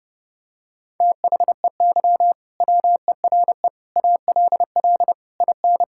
Morse code, heard as a keyed tone, about 30 wpm, 700 Hz. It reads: THEY WERE ALL IN
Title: Sentences from Top 1600 Words 30wpm